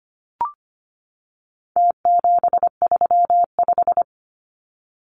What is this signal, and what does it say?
Morse code, heard as a keyed tone, about 25 wpm, 700 Hz. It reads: T735
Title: Numbers - Contest Style - 4-Digit 25wpm